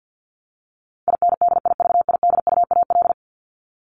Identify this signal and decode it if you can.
Morse code, heard as a keyed tone, about 50 wpm, 700 Hz. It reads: INDIVIDUAL
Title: Top 501-750 Words 50wpm